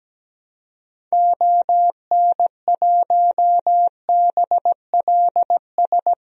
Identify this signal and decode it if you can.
Morse code, heard as a keyed tone, about 17 wpm, 700 Hz. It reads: ON1BLS
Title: DXCC-100 Call Signs - 6 Characters - 17wpm